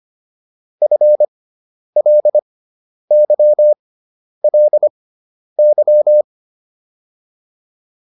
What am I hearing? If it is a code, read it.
Morse code, heard as a keyed tone, about 25 wpm, spaced as for 10 wpm, 600 Hz. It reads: FLYLY